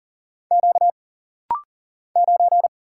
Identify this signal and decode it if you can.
Morse code, heard as a keyed tone, about 44 wpm, 700 Hz. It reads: Q 9
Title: Single Letter-Number Alt - Rapid-Fire 40wpm